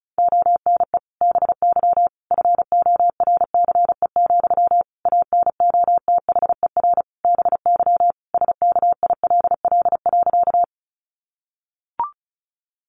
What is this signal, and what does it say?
Morse code, heard as a keyed tone, about 34 wpm, 700 Hz. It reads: ONE BY FORCE, ANOTHER BY SKILL.
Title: Sentences from Top 1200 Words 35wpm